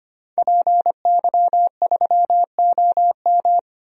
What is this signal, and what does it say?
Morse code, heard as a keyed tone, about 25 wpm, 700 Hz. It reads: PY3OM
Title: DXCC-100 Call Signs - 5 Characters - 25wpm